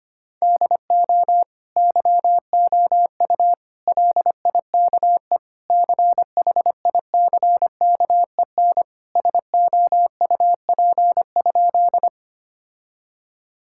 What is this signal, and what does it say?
Morse code, heard as a keyed tone, about 25 wpm, 700 Hz. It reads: DO YOU LIKE CHICKEN SOUP?
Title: Sentences from Top 1000 Words - Review Part 6 - 25wpm